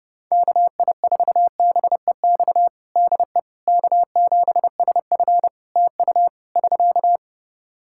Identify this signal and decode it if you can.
Morse code, heard as a keyed tone, about 30 wpm, 700 Hz. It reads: KI4BEX DE K7SF TU <SK>
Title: Practice QSO - Exchange 4 - Ending - 30wpm